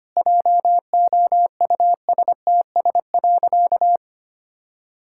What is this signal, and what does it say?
Morse code, heard as a keyed tone, about 25 wpm, 700 Hz. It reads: JOUSTS.
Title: Pangram Challenge v2 - 25wpm